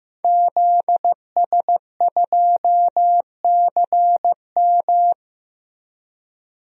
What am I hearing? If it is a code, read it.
Morse code, heard as a keyed tone, about 15 wpm, 700 Hz. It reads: ZS2CM
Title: DXCC-100 Call Signs - 5 Characters - Repeated 2x - 15wpm